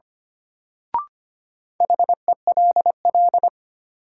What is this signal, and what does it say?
Morse code, heard as a keyed tone, about 25 wpm, 700 Hz. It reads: HELL
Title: Sets of 5 Words Encore - New Word Drill - 25wpm